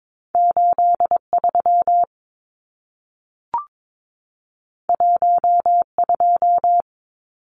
Morse code, heard as a keyed tone, about 22 wpm, 700 Hz. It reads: 83 12
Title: Numbers - 2-Digit 22wpm